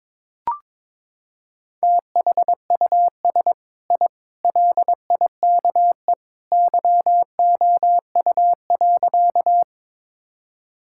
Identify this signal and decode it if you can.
Morse code, heard as a keyed tone, about 22 wpm, 700 Hz. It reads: THUS I LIKE YOU.
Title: Sentences from Top 1000 Words - Review Part 5 - 22wpm